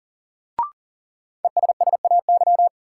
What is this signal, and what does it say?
Morse code, heard as a keyed tone, about 40 wpm, 700 Hz. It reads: ESSAY
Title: Sentences from Top 1500 Words - New Word Drill Alt - 40wpm